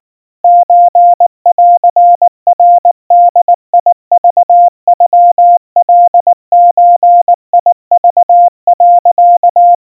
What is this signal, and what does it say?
Morse code (Japanese Wabun code), heard as a keyed tone, about 19 wpm, 700 Hz. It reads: ソンナボクノカゾク、
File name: kawa / Morse sample / kamyu 70 cpm 700 Hz Crispy